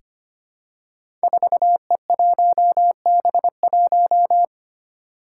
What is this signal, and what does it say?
Morse code, heard as a keyed tone, about 25 wpm, 700 Hz. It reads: 4E1B1
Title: Five Challenging Letter-Number Combinations 25wpm